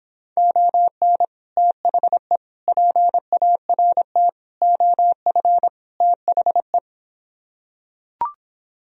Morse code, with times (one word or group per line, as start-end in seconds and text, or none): 0.4-1.2 ON
1.6-2.4 THE
2.7-4.3 PART
4.6-5.7 OF
6.0-6.8 THE